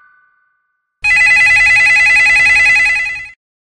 There is no code.